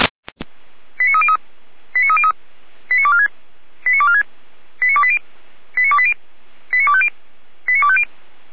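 Five-tone selective call (ZVEI), heard as quick runs of five tones, tones of about 70 ms. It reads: 89303 89303 89246 89246 89280 89280 89250 89250
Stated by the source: ZVEI five-tone selective call received by SDR, from the Feuerwehr Orlamünde fire brigade audio archive.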